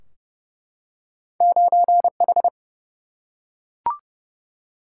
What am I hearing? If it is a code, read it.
Morse code, heard as a keyed tone, about 30 wpm, 700 Hz. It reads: 9 H
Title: Single-Letter Single-Number 30wpm